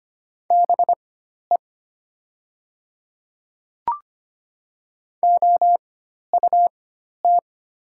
Morse code, heard as a keyed tone, about 25 wpm, spaced as for 12 wpm, 700 Hz. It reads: BE OUT
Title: 4x Character Spacing - Top 100 Words - 25wpm